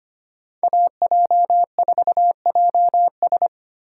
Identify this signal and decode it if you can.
Morse code, heard as a keyed tone, about 25 wpm, 700 Hz. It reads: AJ4JS
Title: USA Call Signs - 5 Characters - 2x Character Spacing - 25wpm